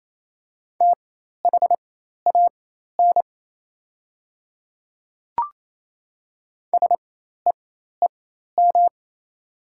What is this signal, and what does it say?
Morse code, heard as a keyed tone, about 28 wpm, spaced as for 13 wpm, 700 Hz. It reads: THAN SEEM
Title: 4x Character Spacing - Top 500 Words - 28wpm